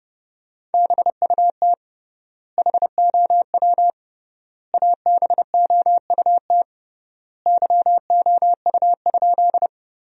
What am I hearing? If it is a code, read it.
Morse code, heard as a keyed tone, about 30 wpm, 700 Hz. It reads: BUT HOW ABOUT YOU?
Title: 3x Word-Spacing - Sentences from Top 100 Words - 30wpm